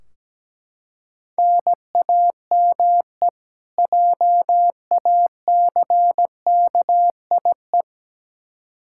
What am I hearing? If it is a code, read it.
Morse code, heard as a keyed tone, about 17 wpm, 700 Hz. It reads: NAME JACKIE
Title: Practice QSO - Exchange 1 - Beginning - 17wpm